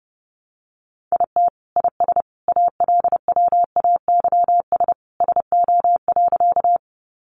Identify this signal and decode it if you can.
Morse code, heard as a keyed tone, about 30 wpm, 700 Hz. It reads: IT IS ALWAYS SO.
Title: Sentences from Top 300 Words 30wpm